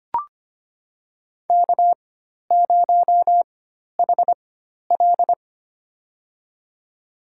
Morse code, heard as a keyed tone, about 25 wpm, spaced as for 12 wpm, 690 Hz. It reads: K0HL
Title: USA Call Signs - 4 Characters - 4x Character Spacing - 25wpm